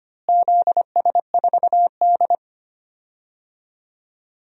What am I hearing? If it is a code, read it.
Morse code, heard as a keyed tone, about 25 wpm, 700 Hz. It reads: ZS4D
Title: DXCC-100 Call Signs - 4 Characters - Repeated 3x - 25wpm